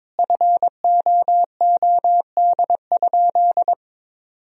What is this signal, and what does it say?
Morse code, heard as a keyed tone, about 22 wpm, 700 Hz. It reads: FOOD?
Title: Sentences from Top 1000 Words - Review Part 2 - 22wpm